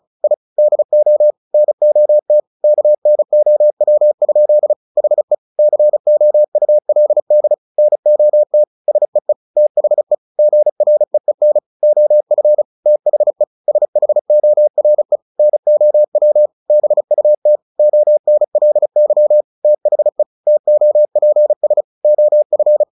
Morse code, heard as a keyed tone, about 34 wpm, 600 Hz. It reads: I DO NOT KNOW? HE COULD NOT SEE THE GREEN OF THE SHORE NOW BUT ONLY THE TOPS OF
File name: the old man and the sea / the old man and the sea 1 35wpm